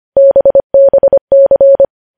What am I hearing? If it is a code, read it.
Morse code, heard as a keyed tone, about 25 wpm, 550 Hz. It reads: BBC
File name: podcast / News headlines-1x-25x25